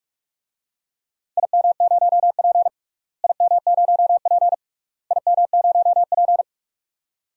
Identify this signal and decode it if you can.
Morse code, heard as a keyed tone, about 46 wpm, 700 Hz. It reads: IM0P IM0P IM0P
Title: DXCC-100 Call Signs - 4 Characters - Repeated 3x - 45wpm